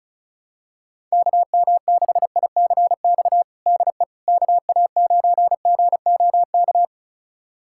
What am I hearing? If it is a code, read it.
Morse code, heard as a keyed tone, about 36 wpm, 700 Hz. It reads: KM6ICX DE KA9GOK
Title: USA Callsign DE Callsign - 6 Characters - 35wpm